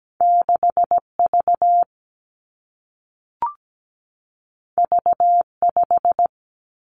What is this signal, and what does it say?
Morse code, heard as a keyed tone, about 17 wpm, 700 Hz. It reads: 6 V V 5
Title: Two Challenging Letter-Number Combinations 17wpm